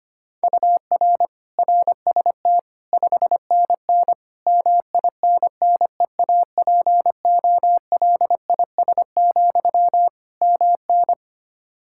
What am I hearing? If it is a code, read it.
Morse code, heard as a keyed tone, about 25 wpm, 700 Hz. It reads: UR RST 5NN MINNEAPOLIS, MN
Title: Practice QSO - Exchange 1 - Beginning - 25wpm